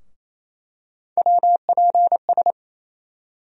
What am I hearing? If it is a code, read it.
Morse code, heard as a keyed tone, about 27 wpm, 700 Hz. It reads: WPS
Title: USA Call Sign Postfixes - 3 Letters 28wpm